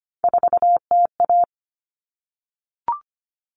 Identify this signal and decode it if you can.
Morse code, heard as a keyed tone, about 25 wpm, 700 Hz. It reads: 4TA
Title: Common Cut Numbers - 3-Digit 25wpm